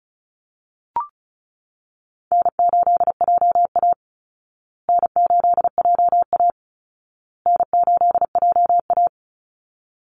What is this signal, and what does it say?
Morse code, heard as a keyed tone, about 34 wpm, 700 Hz. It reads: N8JA N8JA N8JA
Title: Easy Call Signs Repeated 3x - 35wpm